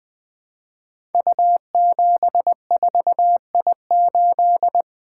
Morse code, heard as a keyed tone, about 20 wpm, 700 Hz. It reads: U74I8
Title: Code Groups - 5 Letter:Numbers - 20wpm